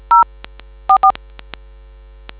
DTMF, heard as telephone touch tones, tones of about 90 ms. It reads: 044